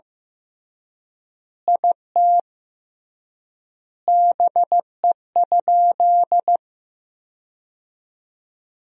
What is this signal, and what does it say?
Morse code, heard as a keyed tone, about 15 wpm, 700 Hz. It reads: IT BE?
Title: 3x Word-Spacing - Sentences from Top 100 Words - 15wpm